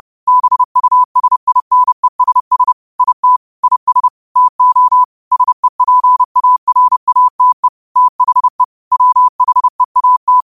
Morse code, heard as a keyed tone, about 30 wpm, 1000 Hz. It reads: BUSINESS IT IS TO SEPARATE THE WHEAT
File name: qotdfiles / 30wpm